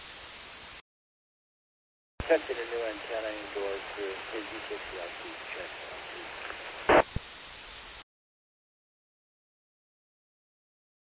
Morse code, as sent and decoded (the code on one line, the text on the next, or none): none